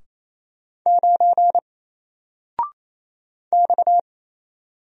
Morse code, heard as a keyed tone, about 27 wpm, 700 Hz. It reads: 9 X
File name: Single-Letter-Number-28wpm